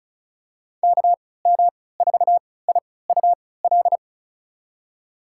Morse code, heard as a keyed tone, about 36 wpm, spaced as for 20 wpm, 700 Hz. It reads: KM4IUL
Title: USA Call Signs - 6 Characters - 3x Character Spacing - 35wpm